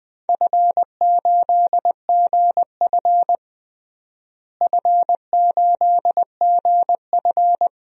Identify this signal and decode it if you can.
Morse code, heard as a keyed tone, about 19 wpm, 700 Hz. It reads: F8GF F8GF
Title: DXCC-100 Call Signs - 4 Characters - Repeated 3x - 20wpm